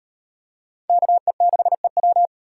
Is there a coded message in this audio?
Yes: Morse code from 0.9 to 2.3 s, heard as a keyed tone, about 38 wpm, 700 Hz. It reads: KE6EW